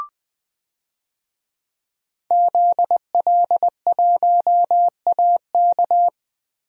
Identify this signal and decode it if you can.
Morse code, heard as a keyed tone, about 20 wpm, 700 Hz. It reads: ZL1AK
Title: DXCC-100 Call Signs - 5 Characters - Repeated 3x - 20wpm